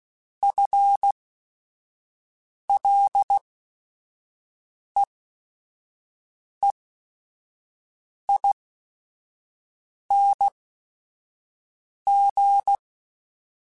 Morse code, heard as a keyed tone, about 16 wpm, spaced as for 5 wpm, 800 Hz. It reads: FLEEING
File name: Chuckles / Chuckles-mp3s / Temptation @5